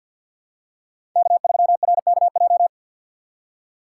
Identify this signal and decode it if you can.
Morse code, heard as a keyed tone, about 53 wpm, 690 Hz. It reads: K3RKJ
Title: USA Call Signs - 5 Characters - 6x Character Spacing - 50wpm